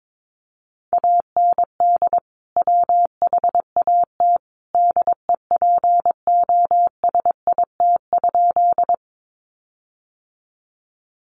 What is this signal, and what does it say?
Morse code, heard as a keyed tone, about 22 wpm, 700 Hz. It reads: AND WHAT DEPOSIT?
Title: Sentences from Top 1300 Words 22wpm